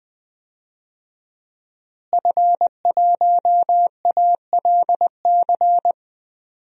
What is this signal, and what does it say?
Morse code, heard as a keyed tone, about 20 wpm, 700 Hz. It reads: F1ALC
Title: DXCC-100 Call Signs - 5 Characters - Repeated 3x - 20wpm